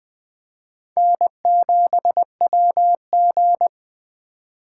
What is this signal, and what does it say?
Morse code, heard as a keyed tone, about 20 wpm, 700 Hz. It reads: N7WG
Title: Speed Racing - USA 1x2 Call Signs 17wpm